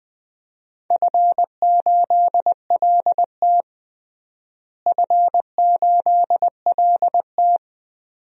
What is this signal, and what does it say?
Morse code, heard as a keyed tone, about 20 wpm, 700 Hz. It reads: F8LT F8LT
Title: DXCC-100 Call Signs - 4 Characters - Repeated 2x - 20wpm